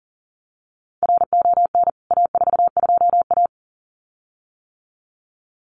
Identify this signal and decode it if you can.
Morse code, heard as a keyed tone, about 40 wpm, 700 Hz. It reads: RON A42A
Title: CWT Exchange 40wpm